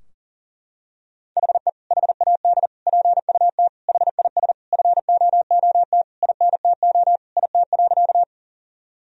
Morse code, heard as a keyed tone, about 37 wpm, 700 Hz. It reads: HE HAD PUT HIS FOOT INTO IT.
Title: Sentences from Top 1000 Words - Review Part 3 - 40wpm